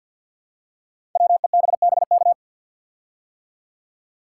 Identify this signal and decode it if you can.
Morse code, heard as a keyed tone, about 52 wpm, 700 Hz. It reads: WEBBK
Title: Code Groups - 5 Letter:Numbers - 50wpm